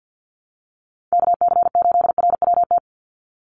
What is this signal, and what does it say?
Morse code, heard as a keyed tone, about 50 wpm, 700 Hz. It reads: KC8RWT
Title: USA Call Signs 50wpm